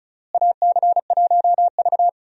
Morse code, heard as a keyed tone, about 34 wpm, 700 Hz. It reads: AC1V
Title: USA Call Signs - 4 Characters - Repeated 3x - 35wpm